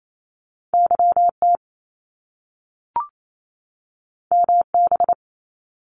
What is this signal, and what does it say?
Morse code, heard as a keyed tone, about 28 wpm, 700 Hz. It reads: YT MB
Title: Canadian Provinces and Territories 28wpm